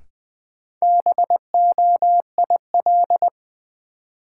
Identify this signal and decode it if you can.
Morse code, heard as a keyed tone, about 20 wpm, 700 Hz. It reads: BOIL
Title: Sets of 5 Words Encore - New Word Drill - 20wpm